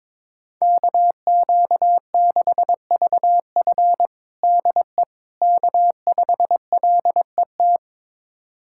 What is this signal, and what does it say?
Morse code, heard as a keyed tone, about 22 wpm, 700 Hz. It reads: KQ6VF DE K5LET
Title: USA Callsign DE Callsign - 5 Characters - 22wpm